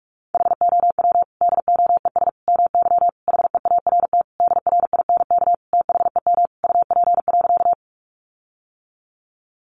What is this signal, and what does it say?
Morse code, heard as a keyed tone, about 44 wpm, 700 Hz. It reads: HOW DOES MY HEART DRINK THEM UP.
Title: Sentences from Top 1500 Words 45wpm